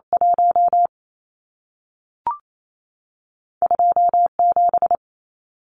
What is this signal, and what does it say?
Morse code, heard as a keyed tone, about 28 wpm, spaced as for 3 wpm, 700 Hz. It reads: 12 7
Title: Numbers - 2-Digit 28wpm